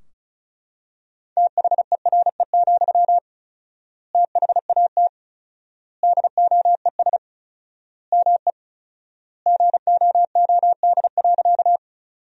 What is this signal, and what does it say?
Morse code, heard as a keyed tone, about 36 wpm, 700 Hz. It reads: THERE, THAT DOES ME GOOD.
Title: 4x Word-Spacing - Sentences from Top 200 Words - 35wpm